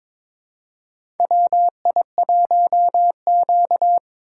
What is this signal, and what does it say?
Morse code, heard as a keyed tone, about 23 wpm, 700 Hz. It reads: WI1Q
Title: USA Call Signs - 4 Characters - 2x Character Spacing - 22wpm